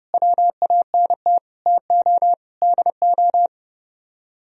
Morse code, heard as a keyed tone, about 33 wpm, 700 Hz. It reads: WANT TO DO